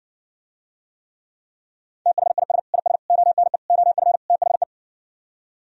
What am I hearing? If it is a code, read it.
Morse code, heard as a keyed tone, about 59 wpm, 700 Hz. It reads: THIS IS ONE OF THE